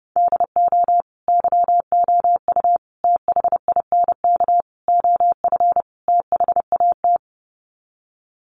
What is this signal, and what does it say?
Morse code, heard as a keyed tone, about 29 wpm, 700 Hz. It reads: DO YOU THINK OF THAT
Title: Sets of 5 Words Encore2 30wpm